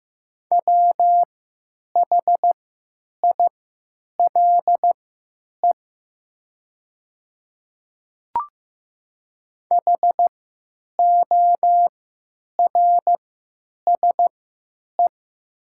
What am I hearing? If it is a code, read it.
Morse code, heard as a keyed tone, about 15 wpm, spaced as for 9 wpm, 700 Hz. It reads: WHILE HORSE
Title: 3x Character Spacing - Top 300 Words - 15wpm